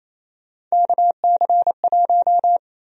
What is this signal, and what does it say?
Morse code, heard as a keyed tone, about 28 wpm, 700 Hz. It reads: KC1
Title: USA Call Sign Prefixes - 28wpm